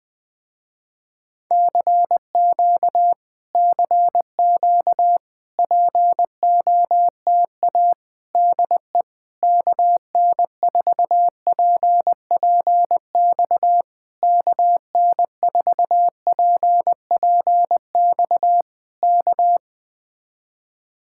Morse code, heard as a keyed tone, about 20 wpm, 700 Hz. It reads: CQ CQ POTA DE KN4PPX KN4PPX K
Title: Speed-Racing Common QSO Phrases Encore - 17wpm